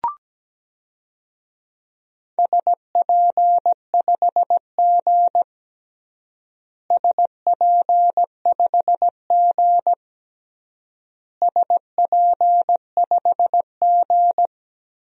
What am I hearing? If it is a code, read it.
Morse code, heard as a keyed tone, about 17 wpm, 700 Hz. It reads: SP5G SP5G SP5G
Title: DXCC-100 Call Signs - 4 Characters - Repeated 3x - 17wpm